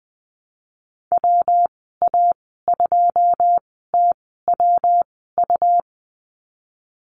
Morse code, heard as a keyed tone, about 20 wpm, spaced as for 15 wpm, 700 Hz. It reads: WA2TWU
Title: 128 Lesson 21g - Number 6 - Call Signs 20wpm